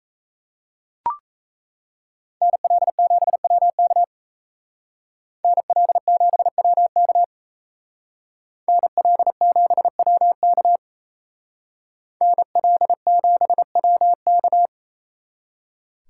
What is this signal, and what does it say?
Morse code, heard as a keyed tone, about 35 wpm, 700 Hz. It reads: NL7WK NL7WK NL7WK NL7WK
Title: Speed Racing - USA 2x2 Call Signs 28wpm